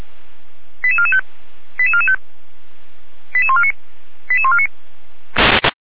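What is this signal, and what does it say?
Five-tone selective call (ZVEI), heard as quick runs of five tones, tones of about 70 ms. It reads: 88445 88445 88149 88149
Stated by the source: ZVEI five-tone selective call received on an SDR.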